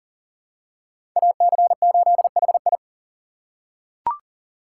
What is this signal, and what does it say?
Morse code, heard as a keyed tone, about 40 wpm, 700 Hz. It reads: AC8HI
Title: USA Call Signs - 2x Character Spacing - 40wpm